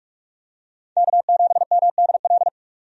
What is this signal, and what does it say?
Morse code, heard as a keyed tone, about 46 wpm, 700 Hz. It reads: K7MDL
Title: USA Call Signs - 5 Characters - 45wpm